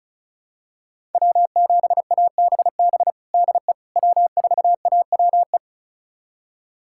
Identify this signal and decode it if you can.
Morse code, heard as a keyed tone, about 34 wpm, 700 Hz. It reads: W7ABB DE W4AWE
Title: USA Callsign DE Callsign - 5 Characters - 35wpm